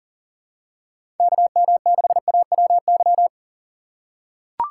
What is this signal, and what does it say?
Morse code, heard as a keyed tone, about 40 wpm, 700 Hz. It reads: KM6AWY
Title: USA Call Signs - 6 Characters - 2x Character Spacing - 40wpm